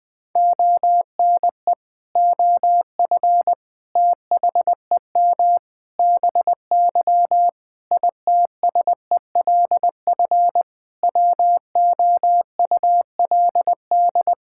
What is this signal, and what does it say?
Morse code, heard as a keyed tone, about 21 wpm, 700 Hz. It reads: ONE OF THEM BY ITSELF WOULD
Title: Sentences from Top 1000 Words - Review Part 5 - 20wpm